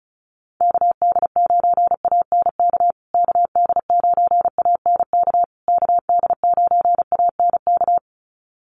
Morse code, heard as a keyed tone, about 34 wpm, 700 Hz. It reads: KD9ANK KD9ANK KD9ANK
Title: USA Call Signs Repeated 3x 35wpm